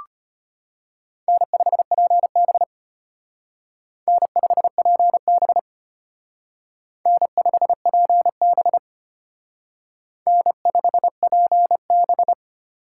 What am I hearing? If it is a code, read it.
Morse code, heard as a keyed tone, about 32 wpm, 700 Hz. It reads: N5PB N5PB N5PB N5PB